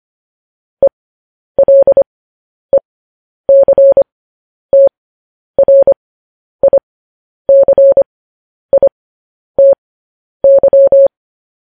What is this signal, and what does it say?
Morse code, heard as a keyed tone, about 25 wpm, spaced as for 10 wpm, 550 Hz. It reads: ELECTRICITY
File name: podcast / News headlines-1x-25x10